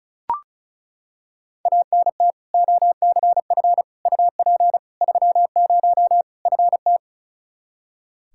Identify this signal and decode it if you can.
Morse code, heard as a keyed tone, about 36 wpm, 700 Hz. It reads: ANT OCF UP 30 FT
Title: Practice QSO - Exchange 2 - Info - 35wpm